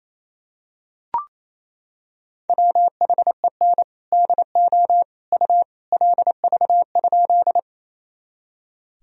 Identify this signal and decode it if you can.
Morse code, heard as a keyed tone, about 28 wpm, 700 Hz. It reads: WHEN DO U LV?
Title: Sentences from Top 400 Words and Abbreviations 28wpm